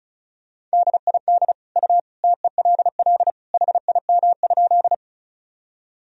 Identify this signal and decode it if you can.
Morse code, heard as a keyed tone, about 36 wpm, 700 Hz. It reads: DID U TELL HIM?
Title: Sentences from Top 200 Words and Abbreviations 35wpm